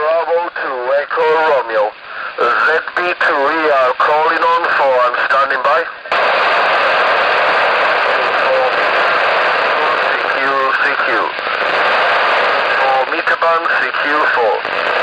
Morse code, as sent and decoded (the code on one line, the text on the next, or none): none